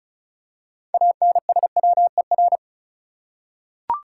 Morse code, heard as a keyed tone, about 34 wpm, 700 Hz. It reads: ANSWER
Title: Six Letter Words - 35wpm